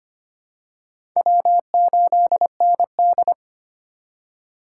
Morse code, heard as a keyed tone, about 25 wpm, 700 Hz. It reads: W8ND
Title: USA Call Signs - 4 Characters - 2x Character Spacing - 25wpm